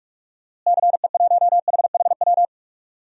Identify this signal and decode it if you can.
Morse code, heard as a keyed tone, about 47 wpm, 700 Hz. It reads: CE1HHW
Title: DXCC-100 Call Signs - 6 Characters - Repeated 3x - 45wpm